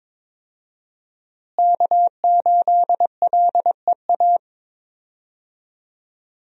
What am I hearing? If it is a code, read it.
Morse code, heard as a keyed tone, about 22 wpm, 700 Hz. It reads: K8LEA
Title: USA Call Signs - 5 Characters - Repeated 3x - 22wpm